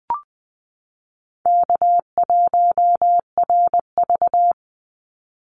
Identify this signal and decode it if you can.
Morse code, heard as a keyed tone, about 20 wpm, 700 Hz. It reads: K1RV
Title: Call Signs Repeated 3x - 20wpm